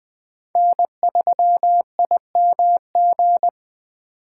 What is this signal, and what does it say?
Morse code, heard as a keyed tone, about 20 wpm, 700 Hz. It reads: N3IMG
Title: Speed Racing - USA 1x3 Call Signs 17wpm